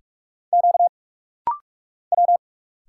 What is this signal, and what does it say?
Morse code, heard as a keyed tone, about 46 wpm, 700 Hz. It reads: Q W
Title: Single Letters - Rapid-Fire 45wpm